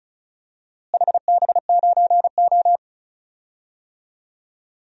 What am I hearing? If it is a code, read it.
Morse code, heard as a keyed tone, about 36 wpm, 700 Hz. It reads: HB9O
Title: DXCC-100 Call Signs - 4 Characters - Repeated 3x - 35wpm